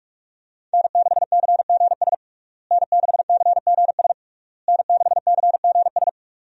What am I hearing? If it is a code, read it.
Morse code, heard as a keyed tone, about 46 wpm, 700 Hz. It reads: N6CGS N6CGS N6CGS
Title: USA Call Signs - 5 Characters - Repeated 3x - 45wpm